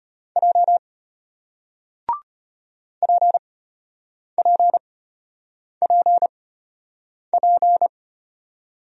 Morse code, heard as a keyed tone, about 33 wpm, spaced as for 8 wpm, 690 Hz. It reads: J PPPP